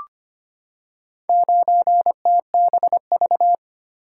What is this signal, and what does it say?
Morse code, heard as a keyed tone, about 25 wpm, 700 Hz. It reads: 9TBV
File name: practice / 25wpm / Four Letter-Number Combinations - 25wpm